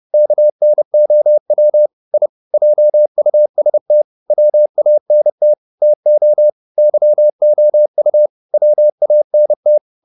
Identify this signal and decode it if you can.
Morse code, heard as a keyed tone, about 30 wpm, 600 Hz. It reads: KNOW I JUST WANT TO YOU WANT